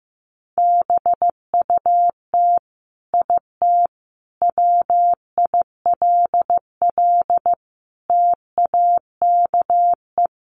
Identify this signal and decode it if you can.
Morse code, heard as a keyed tone, about 14 wpm, 700 Hz. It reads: BUT IT WILL TAKE
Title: Sentences from Top 1300 Words 15wpm